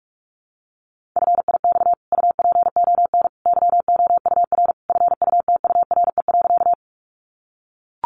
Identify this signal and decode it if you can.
Morse code, heard as a keyed tone, about 44 wpm, 700 Hz. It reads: FIX UPON YOUR FUTURE.
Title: Sentences from Top 1400 Words 45wpm